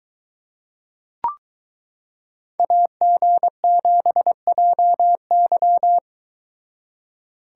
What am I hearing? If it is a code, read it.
Morse code, heard as a keyed tone, about 22 wpm, 700 Hz. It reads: AG7JY